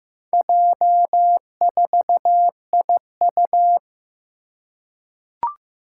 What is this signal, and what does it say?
Morse code, heard as a keyed tone, about 15 wpm, 700 Hz. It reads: J4IU
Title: Four Challenging Letter-Number Combinations 15wpm